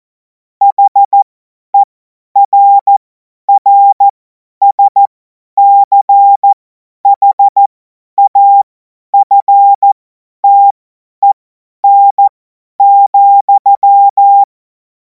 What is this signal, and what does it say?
Morse code, heard as a keyed tone, about 14 wpm, spaced as for 10 wpm, 800 Hz. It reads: HERRSCHAFTEN,